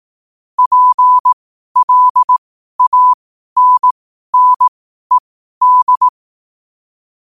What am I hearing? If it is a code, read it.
Morse code, heard as a keyed tone, about 18 wpm, spaced as for 12 wpm, 1000 Hz. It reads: PLANNED
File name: qotdfiles / 12wpm